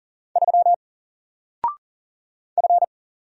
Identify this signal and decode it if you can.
Morse code, heard as a keyed tone, about 40 wpm, 700 Hz. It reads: 3 F
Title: Single Letter-Number 40wpm